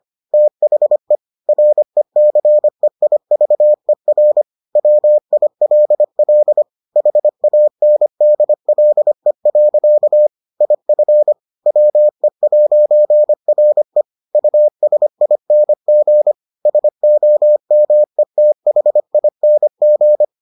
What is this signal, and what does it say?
Morse code, heard as a keyed tone, about 25 wpm, 600 Hz. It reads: THE RECEIVER WILL HANDLE. IF WE'RE USING SOMETHING